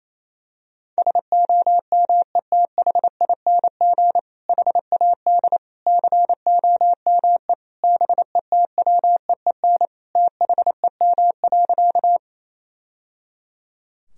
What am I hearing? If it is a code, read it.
Morse code, heard as a keyed tone, about 28 wpm, 700 Hz. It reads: SOMETHING HAD COME BETWEEN THEM.
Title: Sentences from Top 1000 Words - Review Part 1 - 28wpm